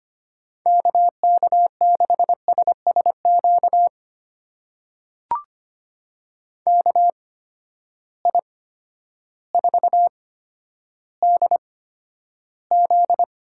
Morse code, heard as a keyed tone, about 25 wpm, spaced as for 11 wpm, 700 Hz. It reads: KK6SSQ KI4DZ
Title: USA Call Signs - 6 Characters - 8x Character Spacing - 25wpm